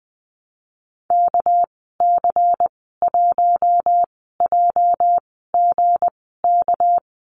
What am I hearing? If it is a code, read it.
Morse code, heard as a keyed tone, about 20 wpm, spaced as for 14 wpm, 700 Hz. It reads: KC1JGK
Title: 213 Lesson 34g - Number 8 - Call Signs 20wpm